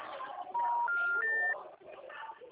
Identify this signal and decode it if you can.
Special information tone, with speech.